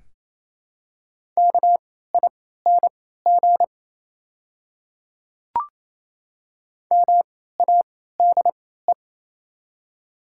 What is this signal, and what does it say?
Morse code, heard as a keyed tone, about 28 wpm, spaced as for 16 wpm, 690 Hz. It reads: KING MADE
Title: 3x Character Spacing - Top 500 Words - 28wpm